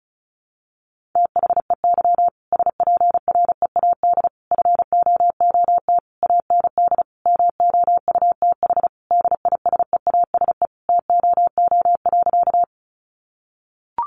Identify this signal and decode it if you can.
Morse code, heard as a keyed tone, about 34 wpm, 700 Hz. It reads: THEY SPREAD FOOT AND MOUTH DISEASE TOO.
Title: Sentences from Top 1500 Words 35wpm